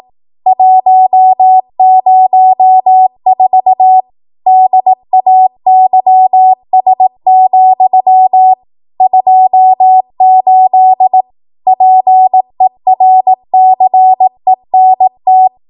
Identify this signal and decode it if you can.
Morse code, heard as a keyed tone, about 18 wpm, 750 Hz. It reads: 104 DAYS, 28 PERCENT
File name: Archive / 18 WPM / 18WPM